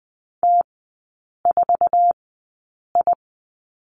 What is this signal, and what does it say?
Morse code, heard as a keyed tone, about 20 wpm, spaced as for 8 wpm, 700 Hz. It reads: T4I
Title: 047 Lesson 9e - Number 4 - Three Characters 20wpm